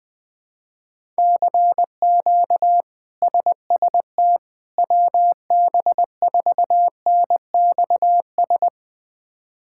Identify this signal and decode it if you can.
Morse code, heard as a keyed tone, about 21 wpm, 700 Hz. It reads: CQ SST WB4NXS
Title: K1USN SST QSO Practice - 20wpm